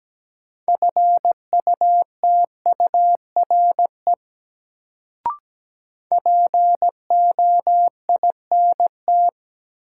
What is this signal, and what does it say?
Morse code, heard as a keyed tone, about 17 wpm, 700 Hz. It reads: FUTURE POINT
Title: Sets of 3 Words - New Word Drill - 17wpm